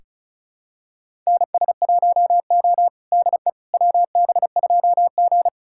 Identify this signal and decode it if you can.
Morse code, heard as a keyed tone, about 34 wpm, 700 Hz. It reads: NS1O DE WB2G
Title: USA Callsign DE Callsign - 4 Characters - 35wpm